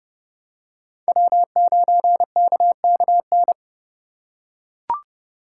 Morse code, heard as a keyed tone, about 32 wpm, 700 Hz. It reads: W9KKN
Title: USA Call Signs - 5 Characters - 8x Character Spacing - 30wpm